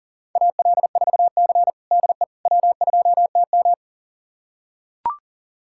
Morse code, heard as a keyed tone, about 40 wpm, 700 Hz. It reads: AL4C DE W2TM
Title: USA Callsign DE Callsign - 4 Characters - 40wpm